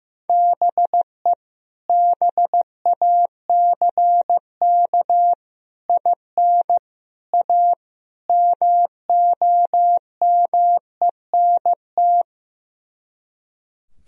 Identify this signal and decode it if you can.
Morse code, heard as a keyed tone, about 16 wpm, 690 Hz. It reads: BE BACK IN A MOMENT
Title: Speed-Racing - Sets of 5 Words 15wpm